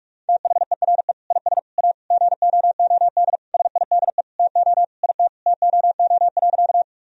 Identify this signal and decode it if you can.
Morse code, heard as a keyed tone, about 46 wpm, 700 Hz. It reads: THERE IS A GOOD SIDE TO IT TOO.
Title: Sentences from Top 1000 Words - Review Part 1 - 45wpm